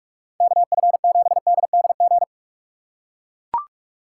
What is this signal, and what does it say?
Morse code, heard as a keyed tone, about 44 wpm, 700 Hz. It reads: KF7DDG